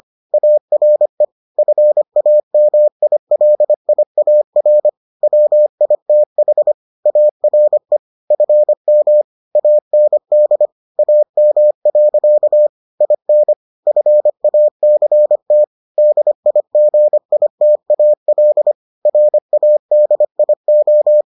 Morse code, heard as a keyed tone, about 25 wpm, 600 Hz. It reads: ARE FAMILIAR WITH ARE FM AND AM. IN FACT DIGITAL RADIO